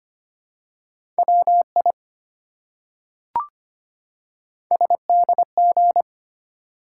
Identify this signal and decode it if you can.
Morse code, heard as a keyed tone, about 25 wpm, 700 Hz. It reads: WI SDG
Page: morse-code-ninja-practice-25wpm